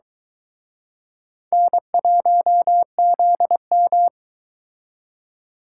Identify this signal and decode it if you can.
Morse code, heard as a keyed tone, about 23 wpm, 700 Hz. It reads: N1ZM